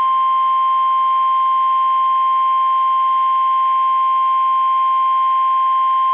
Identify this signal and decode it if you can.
Emergency Alert System (SAME) tones (the 1050 Hz attention tone). It is heard from 0.9 to 5.4 s.
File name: eas / KUSO / KUSO